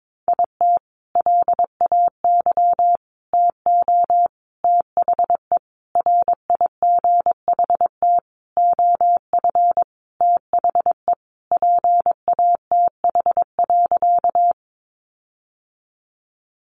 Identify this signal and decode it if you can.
Morse code, heard as a keyed tone, about 22 wpm, 700 Hz. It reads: IT LAY TO THE RIGHT OF THE PATH.
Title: Sentences from Top 1400 Words 22wpm